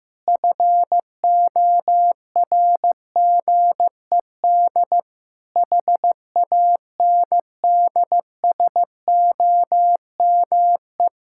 Morse code, heard as a keyed tone, about 15 wpm, 700 Hz. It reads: FORGED HANDSOME
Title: Pangram Challenge v2 - 15wpm